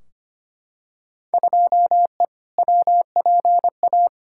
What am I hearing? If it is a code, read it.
Morse code, heard as a keyed tone, about 25 wpm, 700 Hz. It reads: 2E WPA
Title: Field Day Exchanges Encore - 25wpm